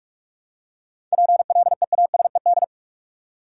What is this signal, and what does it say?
Morse code, heard as a keyed tone, about 44 wpm, 700 Hz. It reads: PLEASED
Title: Seven Letter Words - 45wpm